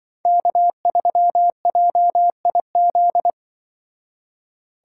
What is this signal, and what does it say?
Morse code, heard as a keyed tone, about 24 wpm, 700 Hz. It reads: K3JIZ
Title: Speed Racing - USA Call Signs 20wpm